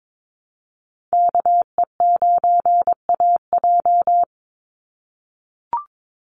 Morse code, heard as a keyed tone, about 22 wpm, 700 Hz. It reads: KE9AJ
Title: NA SOTA Chaser Callsigns 22wpm